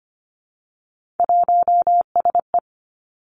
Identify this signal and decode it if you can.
Morse code, heard as a keyed tone, about 25 wpm, 700 Hz. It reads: 1SE
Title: Three Challenging Letter-Number Combinations 25wpm